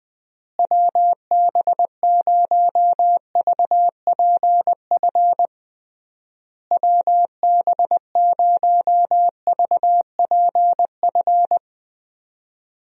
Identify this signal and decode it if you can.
Morse code, heard as a keyed tone, about 20 wpm, 700 Hz. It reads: WB0VPF WB0VPF
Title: USA Call Signs - 6 Characters - Repeated 3x - 20wpm